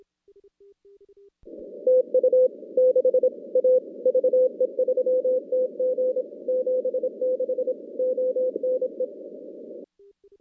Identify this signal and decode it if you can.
Morse code, heard as a keyed tone, about 26 wpm, 510 Hz. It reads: TU 6 A VE3TG 76 ONE
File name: contests / CW-SS-k5tr- -2100